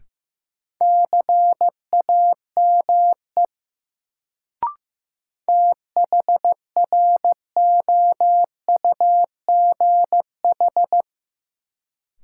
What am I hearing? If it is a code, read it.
Morse code, heard as a keyed tone, about 15 wpm, 700 Hz. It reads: CAME THROUGH
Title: Sentences from Top 200 Words - New Word Drill - 15wpm